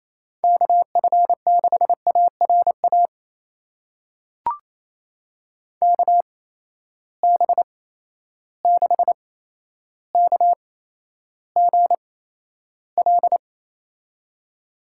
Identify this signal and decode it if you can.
Morse code, heard as a keyed tone, about 28 wpm, spaced as for 12 wpm, 700 Hz. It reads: KF6ARA KB6KGL
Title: USA Call Signs - 6 Characters - 8x Character Spacing - 28wpm